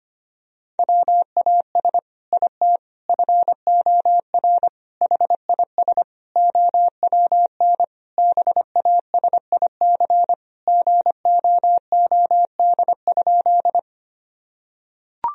Morse code, heard as a keyed tone, about 25 wpm, 700 Hz. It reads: WAS IT FOR HIS OWN BASIC GOOD?
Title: Sentences from Top 1000 Words - Review Part 3 - 25wpm